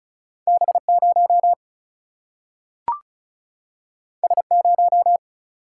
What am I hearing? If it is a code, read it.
Morse code, heard as a keyed tone, about 36 wpm, 700 Hz. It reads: B0 S0
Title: Single-Letter Single-Number 35wpm